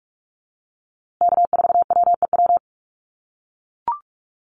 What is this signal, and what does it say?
Morse code, heard as a keyed tone, about 44 wpm, 700 Hz. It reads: K4WEW
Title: USA Call Signs Repeated 2x 45wpm